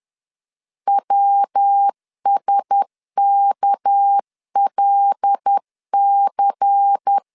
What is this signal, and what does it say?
Morse code, heard as a keyed tone, about 10 wpm, 800 Hz. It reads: WSKLC
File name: LEZIONI / 6 PKX0 / 6 PK 10wpm 2436